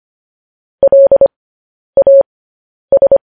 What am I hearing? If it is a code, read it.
Morse code, heard as a keyed tone, about 25 wpm, spaced as for 10 wpm, 550 Hz. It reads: LAS
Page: News Headlines in Morse Code at 10 WPM